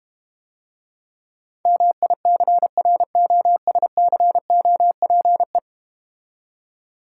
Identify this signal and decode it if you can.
Morse code, heard as a keyed tone, about 32 wpm, 700 Hz. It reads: MICROSCOPE